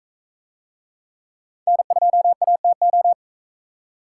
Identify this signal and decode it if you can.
Morse code, heard as a keyed tone, about 44 wpm, 700 Hz. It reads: N2ATO